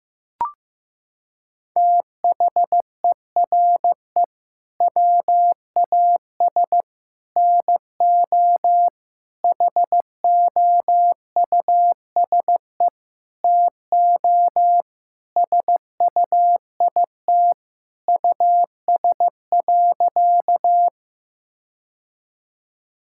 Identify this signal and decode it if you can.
Morse code, heard as a keyed tone, about 15 wpm, 700 Hz. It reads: THERE WAS NO HOUSE TO SUIT US.
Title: Sentences from Top 1600 Words 15wpm